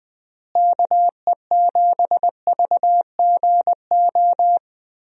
Morse code, heard as a keyed tone, about 20 wpm, 700 Hz. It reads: KE7VGO
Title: USA Call Signs - 6 Characters - Repeated 3x - 20wpm